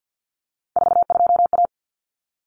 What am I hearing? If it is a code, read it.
Morse code, heard as a keyed tone, about 50 wpm, 700 Hz. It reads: 42A
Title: Common Cut Numbers - 3-Digit 50wpm